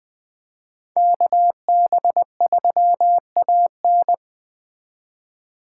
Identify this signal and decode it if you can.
Morse code, heard as a keyed tone, about 20 wpm, 700 Hz. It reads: KB3AN
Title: Speed Racing - USA 2x2 Call Signs 17wpm